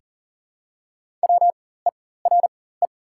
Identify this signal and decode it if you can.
Morse code, heard as a keyed tone, about 40 wpm, spaced as for 19 wpm, 690 Hz. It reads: WERE
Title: 4x Character Spacing - Top 200 Words - 40wpm